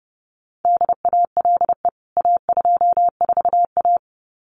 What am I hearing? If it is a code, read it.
Morse code, heard as a keyed tone, about 30 wpm, 700 Hz. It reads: DALE A24A
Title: CWT Exchange 30wpm